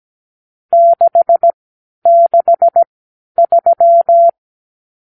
Morse code, heard as a keyed tone, about 17 wpm, spaced as for 11 wpm, 680 Hz. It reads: 663
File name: arrl / Review (1-0)